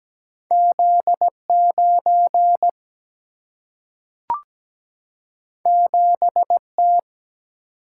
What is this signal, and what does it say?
Morse code, heard as a keyed tone, about 17 wpm, 700 Hz. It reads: Z9 7T
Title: Single-Letter Single-Number 17wpm